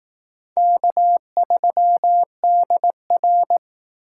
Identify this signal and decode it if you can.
Morse code, heard as a keyed tone, about 19 wpm, 700 Hz. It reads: K3DR